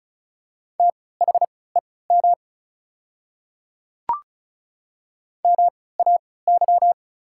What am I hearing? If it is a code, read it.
Morse code, heard as a keyed tone, about 36 wpm, spaced as for 20 wpm, 690 Hz. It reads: THEM MAY
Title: 3x Character Spacing - Top 100 Words - 35wpm